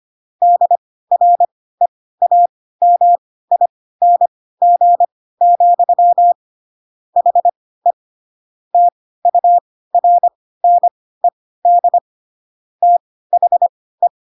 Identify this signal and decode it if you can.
Morse code, heard as a keyed tone, about 25 wpm, spaced as for 16 wpm, 700 Hz. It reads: DREAMING, HE TURNED THE
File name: Pinocchio / Pinocchio ch5 10 25 ik1hge